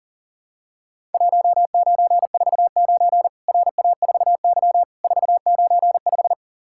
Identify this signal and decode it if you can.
Morse code, heard as a keyed tone, about 40 wpm, 700 Hz. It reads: 1949 RA4Y 495
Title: Worked All Europe DX Contest - QTC Exchange - 40wpm